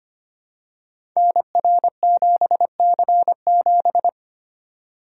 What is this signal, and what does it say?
Morse code, heard as a keyed tone, about 25 wpm, 700 Hz. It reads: NR7C7
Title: Code Groups - 5 Letter:Numbers - 25wpm